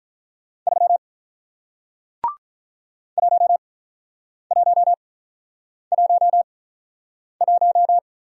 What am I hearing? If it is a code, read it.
Morse code, heard as a keyed tone, about 47 wpm, spaced as for 9 wpm, 700 Hz. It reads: 3 1111